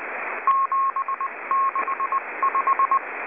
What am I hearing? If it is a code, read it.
Morse code, heard as a keyed tone, about 22 wpm, spaced as for 18 wpm, 1080 Hz, with radio noise. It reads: 765